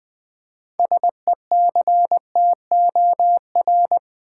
Morse code, heard as a keyed tone, about 21 wpm, 700 Hz. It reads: SECTOR
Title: Six Letter Words - 20wpm